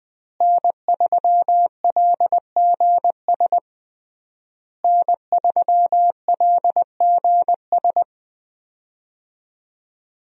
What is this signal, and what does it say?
Morse code, heard as a keyed tone, about 20 wpm, 700 Hz. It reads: N3LGS N3LGS
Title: USA Call Signs - 5 Characters - Repeated 3x - 20wpm